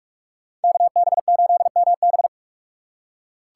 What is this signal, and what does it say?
Morse code, heard as a keyed tone, about 44 wpm, 700 Hz. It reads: KB8MB